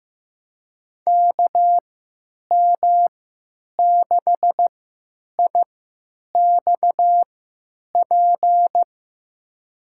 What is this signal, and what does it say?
Morse code, heard as a keyed tone, about 16 wpm, spaced as for 9 wpm, 700 Hz. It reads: KM6IXP
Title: USA Call Signs - 3x Character Spacing - 15wpm